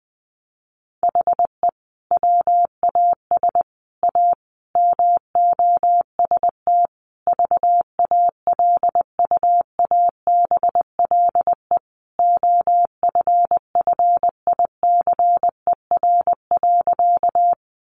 Morse code, heard as a keyed tone, about 19 wpm, 700 Hz. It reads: HE WAS A MOST VALUABLE OFFICER.
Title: Sentences from Top 1200 Words 20wpm